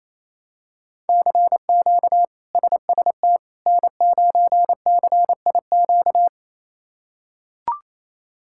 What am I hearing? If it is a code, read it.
Morse code, heard as a keyed tone, about 28 wpm, 700 Hz. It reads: CQ SST N9CIQ
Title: K1USN SST QSO Practice - 28wpm